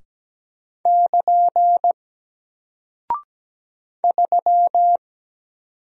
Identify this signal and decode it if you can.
Morse code, heard as a keyed tone, about 17 wpm, 700 Hz. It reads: <KN> 3
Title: Single Character (Letter:Number:Punctuation:Prosign) Alt 17wpm